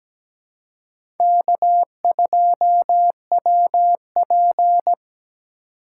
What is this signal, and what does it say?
Morse code, heard as a keyed tone, about 17 wpm, 700 Hz. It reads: K2WP
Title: USA Call Signs - 4 Characters - 4x Character Spacing - 17wpm